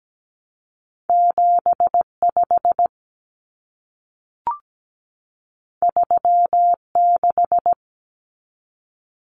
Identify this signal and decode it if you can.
Morse code, heard as a keyed tone, about 17 wpm, 700 Hz. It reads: 75 36
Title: Common Cut Numbers - 2-Digit 17wpm